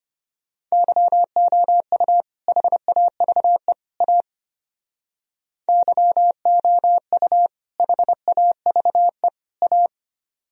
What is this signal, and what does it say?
Morse code, heard as a keyed tone, about 32 wpm, 700 Hz. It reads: YOU HAVE A YOU HAVE A